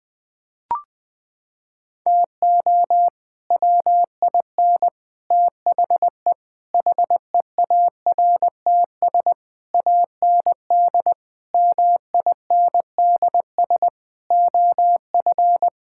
Morse code, heard as a keyed tone, about 21 wpm, 700 Hz. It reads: TO WIN THE HEARTS AND MINDS OF
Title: Sets of 8 Words - 20wpm